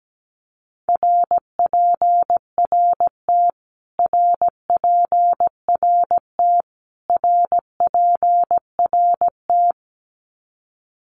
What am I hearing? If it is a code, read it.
Morse code, heard as a keyed tone, about 17 wpm, 700 Hz. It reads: RPRT RPRT RPRT
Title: Abbreviations Repeated 3x - 17wpm